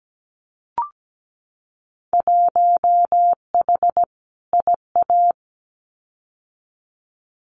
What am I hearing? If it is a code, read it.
Morse code, heard as a keyed tone, about 17 wpm, 700 Hz. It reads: 1H IA
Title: Winter Field Day Exchanges 17wpm